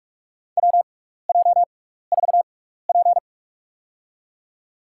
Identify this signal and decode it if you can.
Morse code, heard as a keyed tone, about 46 wpm, spaced as for 15 wpm, 700 Hz. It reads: WJ4P
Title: USA Call Signs - 6x Character Spacing - 45wpm